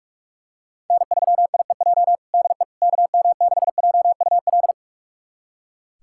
Morse code, heard as a keyed tone, about 41 wpm, 700 Hz. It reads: N3IEJ DE KM6JUL
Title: USA Callsign DE Callsign 45wpm